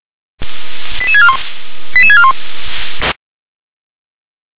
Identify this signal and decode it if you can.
Five-tone selective call (ZVEI), heard as quick runs of five tones, tones of about 70 ms. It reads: 88641 88641